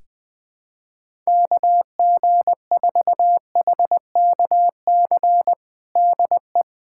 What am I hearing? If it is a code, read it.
Morse code, heard as a keyed tone, about 20 wpm, 700 Hz. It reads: KG4HKC DE
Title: USA Callsign DE Callsign 20wpm